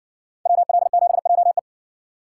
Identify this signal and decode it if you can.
Morse code, heard as a keyed tone, about 55 wpm, 700 Hz. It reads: WL7JE